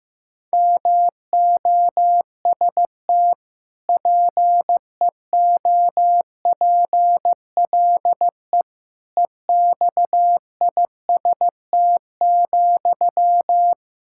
Morse code, heard as a keyed tone, about 15 wpm, 700 Hz. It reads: MOST PEOPLE EXIST,
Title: Sentences from Top 1000 Words - Review Part 5 - 15wpm